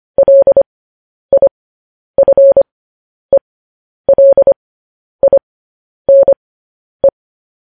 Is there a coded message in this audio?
Yes: Morse code from 0.2 to 7.1 s, heard as a keyed tone, about 25 wpm, spaced as for 10 wpm, 550 Hz. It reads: LIFELINE